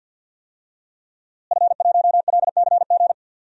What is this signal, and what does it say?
Morse code, heard as a keyed tone, about 50 wpm, 700 Hz. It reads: F1LCG
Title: DXCC-100 Call Signs - 5 Characters - Repeated 3x - 50wpm